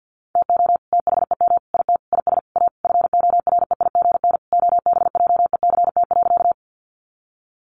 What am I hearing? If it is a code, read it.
Morse code, heard as a keyed tone, about 50 wpm, 700 Hz. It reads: TO THEM IT IS A FOREIGN OBJECT.
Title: Sentences from Top 1400 Words 50wpm